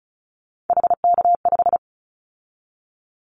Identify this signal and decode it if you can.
Morse code, heard as a keyed tone, about 36 wpm, 700 Hz. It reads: HK5
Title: Callsign Prefixes for Top 100 DXCC - 35wpm